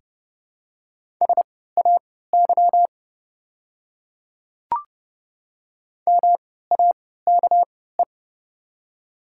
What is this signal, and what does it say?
Morse code, heard as a keyed tone, about 30 wpm, spaced as for 17 wpm, 700 Hz. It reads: SAY MAKE
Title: 3x Character Spacing - Top 200 Words - 30wpm